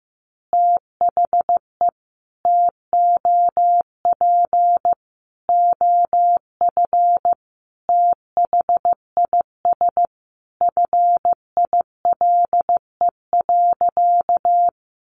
Morse code, heard as a keyed tone, about 15 wpm, 700 Hz. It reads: THE TOP OF THIS FILE.
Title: Sentences from Top 1200 Words 15wpm